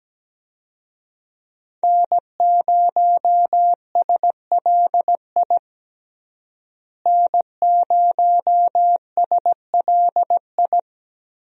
Morse code, heard as a keyed tone, about 17 wpm, 700 Hz. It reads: N0SLI N0SLI
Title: USA Call Signs - 5 Characters - Repeated 2x - 17wpm